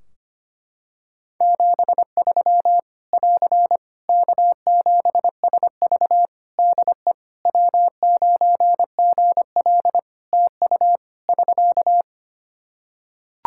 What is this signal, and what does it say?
Morse code, heard as a keyed tone, about 25 wpm, 700 Hz. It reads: 73 <AR> K7SV DE W9GL TU <SK>
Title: Practice QSO - Exchange 4 - Ending - 25wpm